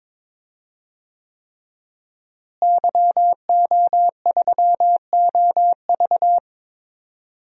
Morse code, heard as a keyed tone, about 22 wpm, 700 Hz. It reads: YO3OV
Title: DXCC-100 Call Signs - 5 Characters - Repeated 3x - 22wpm